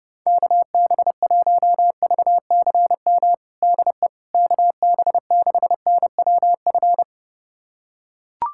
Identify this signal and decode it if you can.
Morse code, heard as a keyed tone, about 30 wpm, 700 Hz. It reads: KB1VCM DE KB6NWF
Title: USA Callsign DE Callsign - 6 Characters - 30wpm